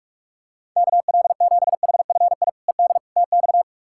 Morse code, heard as a keyed tone, about 41 wpm, 700 Hz. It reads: KL7HFI ED TX
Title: K1USN SST QSO Practice - 45wpm